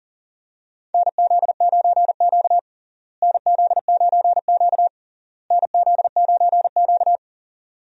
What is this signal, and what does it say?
Morse code, heard as a keyed tone, about 40 wpm, 700 Hz. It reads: NZ9Q NZ9Q NZ9Q
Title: USA Call Signs - 4 Characters - Repeated 3x - 40wpm